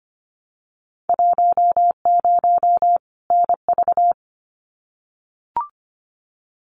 Morse code, heard as a keyed tone, about 25 wpm, 690 Hz. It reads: 10 NV
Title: Winter Field Day Exchanges 25wpm